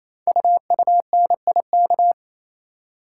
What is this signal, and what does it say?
Morse code, heard as a keyed tone, about 29 wpm, 700 Hz. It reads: UUNIK
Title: Code Groups - 5 Letters - 28wpm